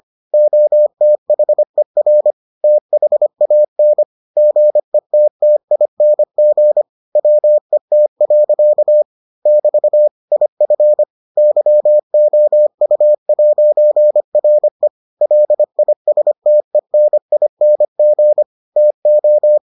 Morse code, heard as a keyed tone, about 25 wpm, 600 Hz. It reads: OTHER THAN GETTING WET. <BT> IF YOU'RE LISTENING TO